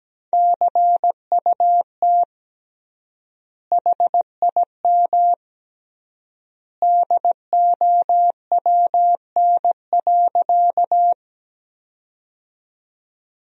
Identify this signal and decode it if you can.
Morse code, heard as a keyed tone, about 17 wpm, 700 Hz. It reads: CUT HIM DOWN.
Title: 3x Word-Spacing - Sentences from Top 300 Words - 17wpm